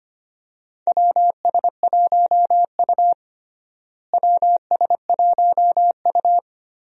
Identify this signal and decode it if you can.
Morse code, heard as a keyed tone, about 25 wpm, 700 Hz. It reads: WS1U WS1U
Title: USA Call Signs - 4 Characters - Repeated 2x - 25wpm